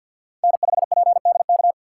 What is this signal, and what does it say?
Morse code, heard as a keyed tone, about 48 wpm, 700 Hz. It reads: N5PDK
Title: USA Call Signs - 5 Characters - 6x Character Spacing - 50wpm